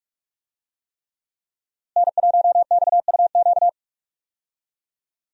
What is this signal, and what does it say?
Morse code, heard as a keyed tone, about 46 wpm, 700 Hz. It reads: N1XUQ
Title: USA Call Signs - 5 Characters - Repeated 3x - 45wpm